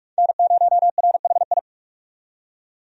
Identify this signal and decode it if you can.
Morse code, heard as a keyed tone, about 43 wpm, 700 Hz. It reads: N0RHI
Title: USA Call Signs - 5 Characters - 45wpm